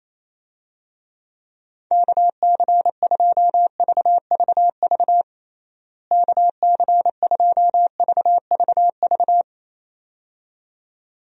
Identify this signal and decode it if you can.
Morse code, heard as a keyed tone, about 28 wpm, 700 Hz. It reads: KC2VVV KC2VVV
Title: USA Call Signs - 6 Characters - Repeated 2x - 28wpm